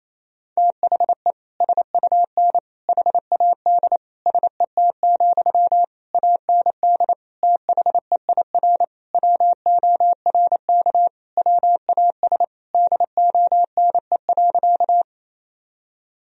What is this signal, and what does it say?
Morse code, heard as a keyed tone, about 28 wpm, 700 Hz. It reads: THE SUN HAD SET, AND THEIR WORK WAS DONE.
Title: Sentences from Top 1000 Words - Review Part 1 - 28wpm